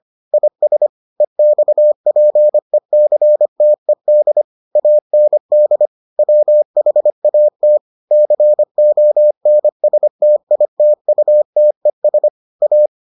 Morse code, heard as a keyed tone, about 25 wpm, 600 Hz. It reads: IS EXPECTED AND WHAT CONSTITUTES A